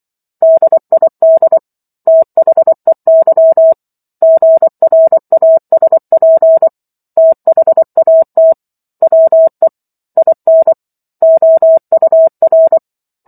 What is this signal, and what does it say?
Morse code, heard as a keyed tone, about 24 wpm, 650 Hz. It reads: DID THEY GRASP THAT WE IN OUR